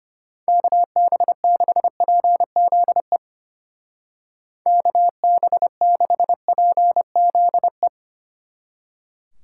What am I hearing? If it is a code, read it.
Morse code, heard as a keyed tone, about 31 wpm, 700 Hz. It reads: KB6PZE KB6PZE